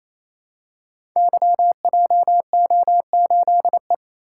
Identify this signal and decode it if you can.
Morse code, heard as a keyed tone, about 27 wpm, 700 Hz. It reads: YJO8E
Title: Code Groups - 5 Letter:Numbers - 28wpm